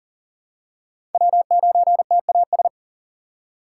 Morse code, heard as a keyed tone, about 40 wpm, 700 Hz. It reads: W9TAS
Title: USA Call Signs - 5 Characters - 3x Character Spacing - 40wpm